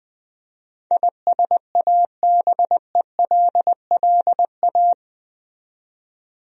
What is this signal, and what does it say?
Morse code, heard as a keyed tone, about 20 wpm, 700 Hz. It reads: ISABELLA
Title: USA Common Names - 20wpm